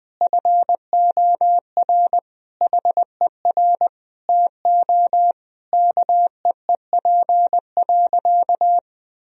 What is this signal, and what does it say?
Morse code, heard as a keyed tone, about 20 wpm, 700 Hz. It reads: FOR HER TO KEEP.
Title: Sentences from Top 1000 Words - Review Part 2 - 20wpm